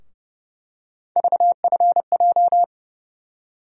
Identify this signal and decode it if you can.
Morse code, heard as a keyed tone, about 30 wpm, 700 Hz. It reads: VFJ
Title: USA Call Sign Postfixes - 3 Letters 30wpm